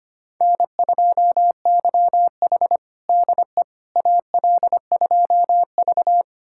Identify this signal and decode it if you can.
Morse code, heard as a keyed tone, about 25 wpm, 700 Hz. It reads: N2YH DE AL2V
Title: USA Callsign DE Callsign - 4 Characters - 25wpm